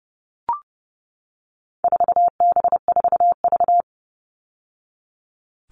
Morse code, heard as a keyed tone, about 30 wpm, 700 Hz. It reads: 4B4V
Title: All Cut Numbers - 4-Digit 30wpm